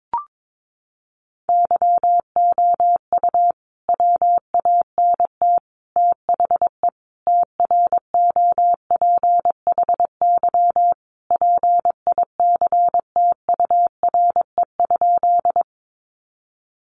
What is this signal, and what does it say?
Morse code, heard as a keyed tone, about 22 wpm, 700 Hz. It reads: YOU WANT THE TROPHY PICTURE?
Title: A Sound Of Thunder 22wpm